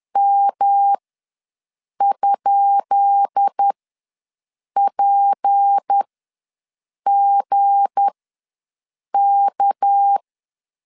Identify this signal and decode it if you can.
Morse code, heard as a keyed tone, about 10 wpm, spaced as for 6 wpm, 800 Hz. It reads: M?PGK
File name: LEZIONI / 10 SIGNS / 10B INT ws 10wpm 4134